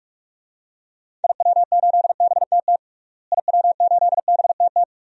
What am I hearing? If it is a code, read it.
Morse code, heard as a keyed tone, about 41 wpm, 700 Hz. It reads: IW8BTT IW8BTT
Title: DXCC-100 Call Signs - 6 Characters - Repeated 3x - 45wpm